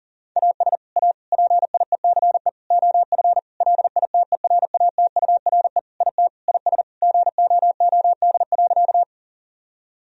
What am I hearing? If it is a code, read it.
Morse code, heard as a keyed tone, about 40 wpm, 700 Hz. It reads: AS A PIECE OF LITERATURE IT IS GOOD.
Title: Sentences from Top 1000 Words - Review Part 3 - 40wpm